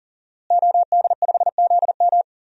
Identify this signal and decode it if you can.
Morse code, heard as a keyed tone, about 39 wpm, 700 Hz. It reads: OD5ZM
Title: DXCC-100 Call Signs - 5 Characters - Repeated 3x - 40wpm